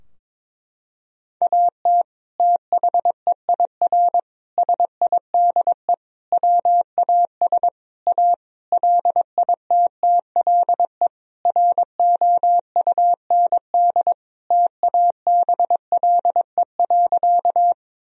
Morse code, heard as a keyed tone, about 22 wpm, 700 Hz. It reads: AT THEIR SIDE WAS A LITTLE ROUND TABLE.
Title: Sentences from Top 1000 Words - Review Part 2 - 22wpm